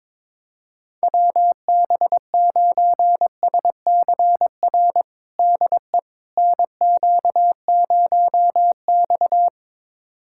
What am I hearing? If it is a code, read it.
Morse code, heard as a keyed tone, about 22 wpm, 700 Hz. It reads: WB9SCR DE NQ0X
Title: USA Callsign DE Callsign 22wpm